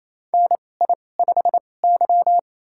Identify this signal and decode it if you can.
Morse code, heard as a keyed tone, about 28 wpm, spaced as for 20 wpm, 700 Hz. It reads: NI5Y
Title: USA Call Signs - 4 Characters - 2x Character Spacing - 28wpm